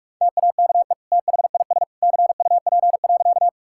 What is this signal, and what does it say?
Morse code, heard as a keyed tone, about 41 wpm, 700 Hz. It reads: TAKE THIS CUP.
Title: Sentences from Top 1600 Words 45wpm